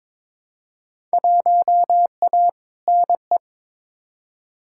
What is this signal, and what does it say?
Morse code, heard as a keyed tone, about 22 wpm, 700 Hz. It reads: 1A NE
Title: Field Day Exchanges Encore - 22wpm